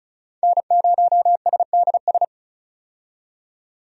Morse code, heard as a keyed tone, about 36 wpm, 700 Hz. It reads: N0SDS
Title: USA Call Signs - 5 Characters - 35wpm